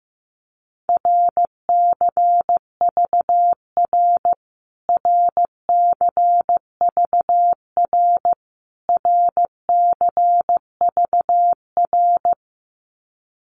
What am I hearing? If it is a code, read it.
Morse code, heard as a keyed tone, about 15 wpm, 700 Hz. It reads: RCVR RCVR RCVR
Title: Abbreviations Repeated 3x - 15wpm